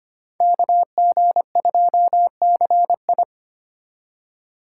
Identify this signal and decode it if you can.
Morse code, heard as a keyed tone, about 32 wpm, 700 Hz. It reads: KG2CI